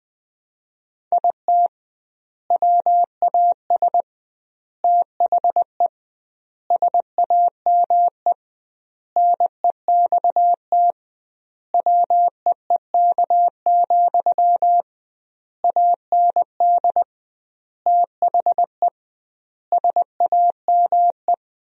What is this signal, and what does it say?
Morse code, heard as a keyed tone, about 20 wpm, 700 Hz. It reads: IT WAS THE SAME NEXT WEEK, AND THE SAME
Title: 2x Word-Spacing - Sentences from Top 500 Words - 20wpm